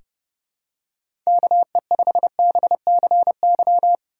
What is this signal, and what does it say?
Morse code, heard as a keyed tone, about 30 wpm, 700 Hz. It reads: KE5BCY
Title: K1USN SST QSO Practice - 30wpm